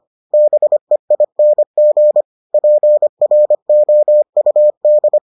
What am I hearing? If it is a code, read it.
Morse code, heard as a keyed tone, about 25 wpm, 600 Hz. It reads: BEING PROUD